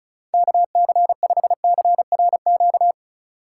Morse code, heard as a keyed tone, about 34 wpm, 700 Hz. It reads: KC5CRQ
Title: USA Call Signs - 6 Characters - 2x Character Spacing - 35wpm